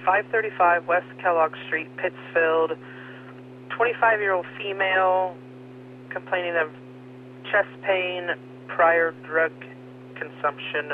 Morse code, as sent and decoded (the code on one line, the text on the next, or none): none